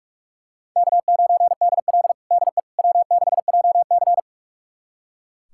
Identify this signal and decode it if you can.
Morse code, heard as a keyed tone, about 45 wpm, 700 Hz. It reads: K9DL DE W6JC
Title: USA Callsign DE Callsign - 4 Characters - 45wpm